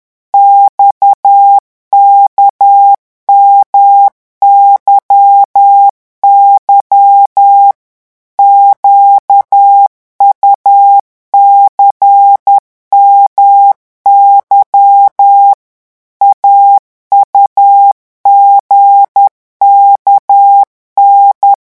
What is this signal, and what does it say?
Morse code, heard as a keyed tone, about 10 wpm, 800 Hz. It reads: XKMYY QUCMY AUGKN
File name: morse mp3 / revision3-vitesse 10 mots